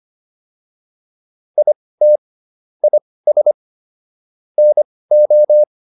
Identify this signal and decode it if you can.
Morse code, heard as a keyed tone, about 25 wpm, spaced as for 18 wpm, 600 Hz. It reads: IT IS NO
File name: PR103 18